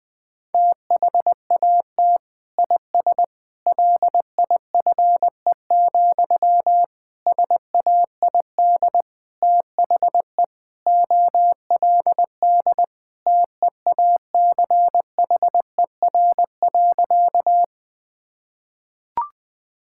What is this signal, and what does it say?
Morse code, heard as a keyed tone, about 20 wpm, 700 Hz. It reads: THAT IS LIFE, SAID THE OLD TEACHER.
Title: Sentences from Top 1000 Words - Review Part 6 - 20wpm